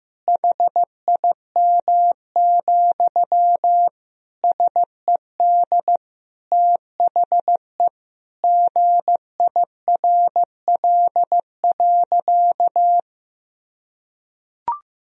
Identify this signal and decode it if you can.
Morse code, heard as a keyed tone, about 16 wpm, 700 Hz. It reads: HIM, SED THE GIRL.
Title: Sentences from Top 400 Words and Abbreviations 15wpm